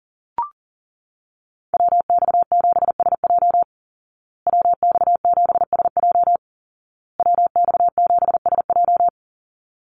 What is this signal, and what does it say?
Morse code, heard as a keyed tone, about 40 wpm, 700 Hz. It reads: WX7SJ WX7SJ WX7SJ
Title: Call Signs Repeated 2x - 40wpm